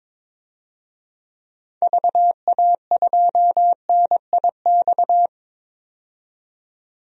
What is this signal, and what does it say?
Morse code, heard as a keyed tone, about 23 wpm, 700 Hz. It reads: VA2NIX
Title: DXCC-100 Call Signs - 6 Characters - 22wpm